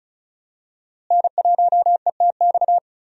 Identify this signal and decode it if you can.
Morse code, heard as a keyed tone, about 36 wpm, 700 Hz. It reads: N1ETX
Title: USA Call Signs - 5 Characters - Repeated 3x - 35wpm